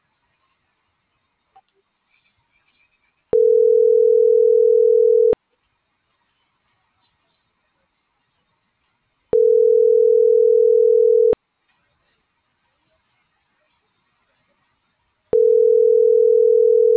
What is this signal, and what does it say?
Ringback tone (North America), 440 + 480 Hz.
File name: CENTRAL TERRESTRE / out els..laz.elo.mo.24